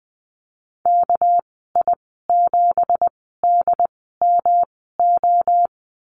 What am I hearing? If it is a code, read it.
Morse code, heard as a keyed tone, about 20 wpm, spaced as for 15 wpm, 700 Hz. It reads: KI7DMO
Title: 231 Lesson 37g - Letter x - Call Signs 20wpm